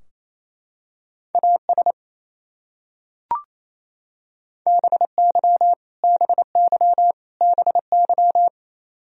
Morse code, heard as a keyed tone, about 28 wpm, 700 Hz. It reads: AS BY BY BY
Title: Top 100 Words Repeated 28wpm